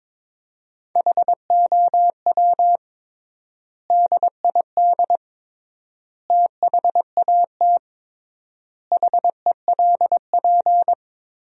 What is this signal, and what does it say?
Morse code, heard as a keyed tone, about 22 wpm, 700 Hz. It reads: HOW DID THAT HELP
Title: 3x Word-Spacing - Sentences from Top 200 Words - 22wpm